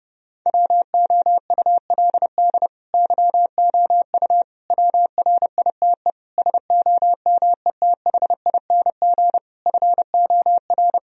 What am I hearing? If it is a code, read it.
Morse code, heard as a keyed tone, about 30 wpm, 700 Hz. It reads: WOULD YOU WRITE SOMETHING FOR
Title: Sentences from Top 1000 Words - Review Part 2 - 30wpm